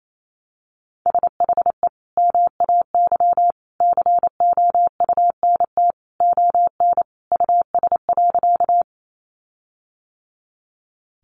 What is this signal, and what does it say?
Morse code, heard as a keyed tone, about 28 wpm, 700 Hz. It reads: SHE MAY COUNT ON US.
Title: Sentences from Top 1500 Words 28wpm